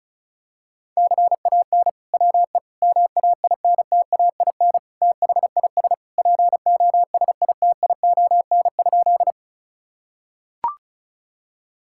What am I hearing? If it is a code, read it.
Morse code, heard as a keyed tone, about 34 wpm, 700 Hz. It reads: CAN WE MAINTAIN THIS POSITION?
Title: Sentences from Top 1000 Words - Review Part 5 - 35wpm